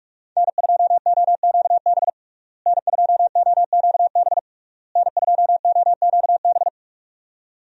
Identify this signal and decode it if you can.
Morse code, heard as a keyed tone, about 41 wpm, 700 Hz. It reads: N2OQB N2OQB N2OQB
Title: USA Call Signs - 5 Characters - Repeated 3x - 45wpm